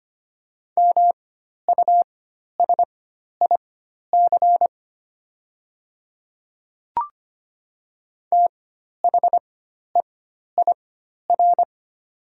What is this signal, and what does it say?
Morse code, heard as a keyed tone, about 25 wpm, spaced as for 12 wpm, 700 Hz. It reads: MUSIC THEIR
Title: 4x Character Spacing - Top 300 Words - 25wpm